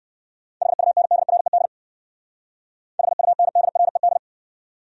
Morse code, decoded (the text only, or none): HUNDRED HUNDRED